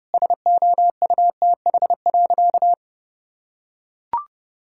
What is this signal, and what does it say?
Morse code, heard as a keyed tone, about 30 wpm, 700 Hz. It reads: SOUTH.
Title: Sentences from Top 1000 Words - Review Part 6 - 30wpm